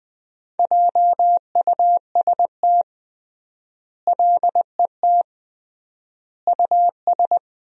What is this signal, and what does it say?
Morse code, heard as a keyed tone, about 20 wpm, 700 Hz. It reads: JUST LET US
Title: 3x Word-Spacing - Sentences from Top 500 Words - 20wpm